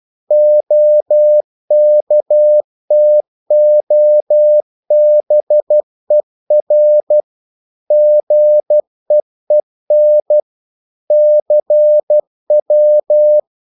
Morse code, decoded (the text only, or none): OKTOBER GEEN CW